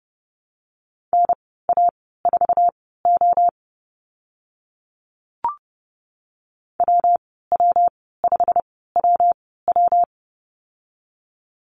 Morse code, heard as a keyed tone, about 30 wpm, spaced as for 17 wpm, 700 Hz. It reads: NA4O WW5WW
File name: Lesson 19g - Letter w - Call Signs 30wpm